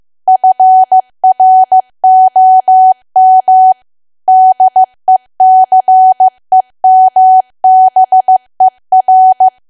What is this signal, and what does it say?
Morse code, heard as a keyed tone, about 15 wpm, 750 Hz. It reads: FROM DECEMBER
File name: cwa / W1AW-Level2 / 15WPM